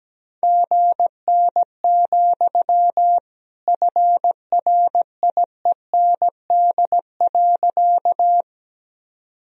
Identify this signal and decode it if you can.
Morse code, heard as a keyed tone, about 17 wpm, 700 Hz. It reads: GN, FRIEND.
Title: Sentences from Top 300 Words and Abbreviations 17wpm